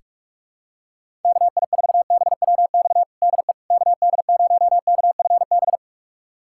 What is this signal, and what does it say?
Morse code, heard as a keyed tone, about 47 wpm, 700 Hz. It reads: KI4BWX DE KD0KFB
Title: USA Callsign DE Callsign - 6 Characters - 45wpm